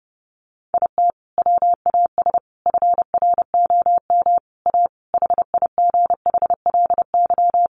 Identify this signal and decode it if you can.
Morse code, heard as a keyed tone, about 30 wpm, 700 Hz. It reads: IT WAS FROM A HIGHLY
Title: Sentences from Top 1100 Words 30wpm